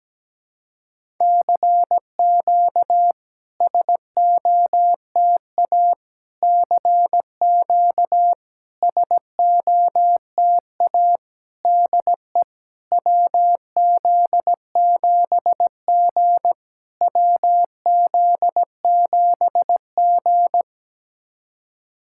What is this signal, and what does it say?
Morse code, heard as a keyed tone, about 17 wpm, 690 Hz. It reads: CQ SOTA CQ SOTA DE WZ7G WZ7G
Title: SOTA QSO Practice 17wpm